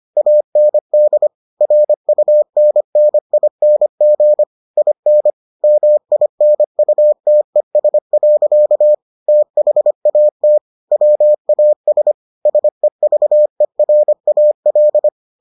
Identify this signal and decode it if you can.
Morse code, heard as a keyed tone, about 25 wpm, 600 Hz. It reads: AND RUNNING IN MINUTES. THAT WAS SEVERAL